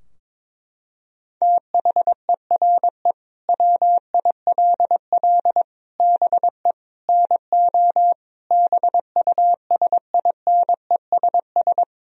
Morse code, heard as a keyed tone, about 22 wpm, 700 Hz. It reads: THERE WILL BE NO BUSINESS
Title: Sentences from Top 1000 Words - Review Part 4 - 22wpm